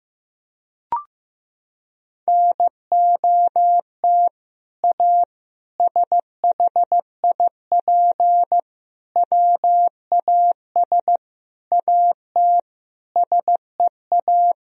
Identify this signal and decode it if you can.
Morse code, heard as a keyed tone, about 15 wpm, 700 Hz. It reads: NOT A SHIP WAS AT SEA
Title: Sentences from Top 1000 Words - Review Part 2 - 15wpm